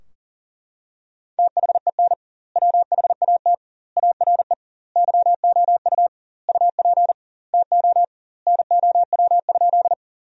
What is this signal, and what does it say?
Morse code, heard as a keyed tone, about 40 wpm, 700 Hz. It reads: THEN WHAT ARE YOU UP TO NOW?
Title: 2x Word-Spacing - Sentences from Top 100 Words - 40wpm